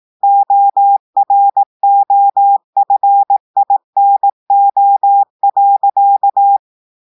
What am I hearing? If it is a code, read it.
Morse code, heard as a keyed tone, about 18 wpm, 800 Hz. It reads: OROFINO.
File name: audios / portugais / portugais texte en clair 1 vitesse 18 mots minutes